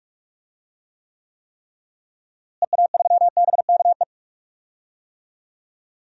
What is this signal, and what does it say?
Morse code, heard as a keyed tone, about 43 wpm, 700 Hz. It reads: EA3BKE
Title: DXCC-100 Call Signs - 6 Characters - Repeated 3x - 45wpm